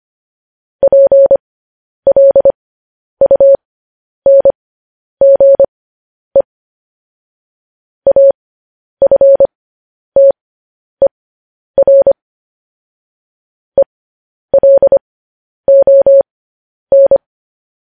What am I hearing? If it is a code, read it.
Morse code, heard as a keyed tone, about 25 wpm, spaced as for 10 wpm, 550 Hz. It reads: PLUNGE AFTER ELON
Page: news headlines in morse code at 10 wpm